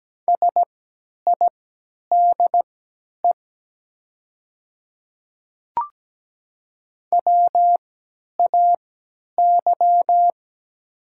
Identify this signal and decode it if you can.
Morse code, heard as a keyed tone, about 17 wpm, spaced as for 10 wpm, 700 Hz. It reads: SIDE WAY
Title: 3x Character Spacing - Top 100 Words - 17wpm